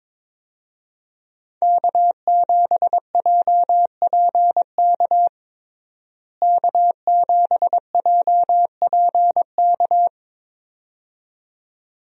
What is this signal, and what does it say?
Morse code, heard as a keyed tone, about 22 wpm, 700 Hz. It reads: K7JPK K7JPK
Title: USA Call Signs - 5 Characters - Repeated 2x - 22wpm